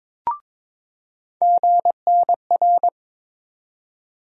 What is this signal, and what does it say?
Morse code, heard as a keyed tone, about 21 wpm, 700 Hz. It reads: GNR
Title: USA Call Sign Postfixes - 3 Letters 22wpm